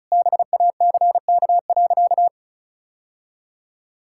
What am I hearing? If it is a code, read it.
Morse code, heard as a keyed tone, about 36 wpm, 700 Hz. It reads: BACK.
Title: Sentences from Top 1000 Words - Review Part 5 - 35wpm